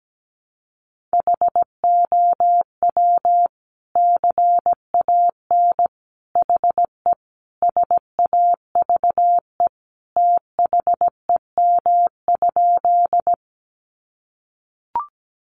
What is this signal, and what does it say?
Morse code, heard as a keyed tone, about 17 wpm, 700 Hz. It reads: HOW CAN HE SAVE THEM?
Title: Sentences from Top 1100 Words 17wpm